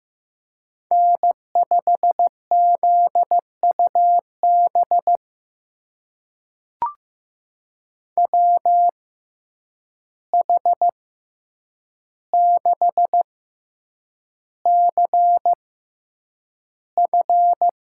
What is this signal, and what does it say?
Morse code, heard as a keyed tone, about 15 wpm, spaced as for 8 wpm, 700 Hz. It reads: N5ZUB WH6CF
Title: USA Call Signs - 5 Characters - 6x Character Spacing - 15wpm